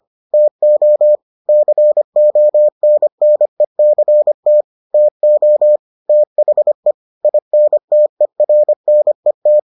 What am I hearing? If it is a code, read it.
Morse code, heard as a keyed tone, about 25 wpm, 600 Hz. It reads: TO CONNECT TO THE INTERNET